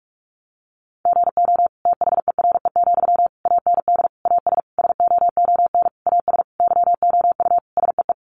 Morse code, heard as a keyed tone, about 44 wpm, 700 Hz. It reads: GO THERE, AND AS SOON AS YOU SEE
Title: Sentences from Top 1300 Words 45wpm